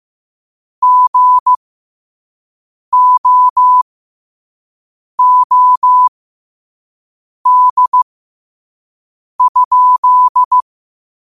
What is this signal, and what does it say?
Morse code, heard as a keyed tone, about 15 wpm, spaced as for 5 wpm, 1000 Hz. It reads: GOOD?
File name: qotdfiles / 5wpm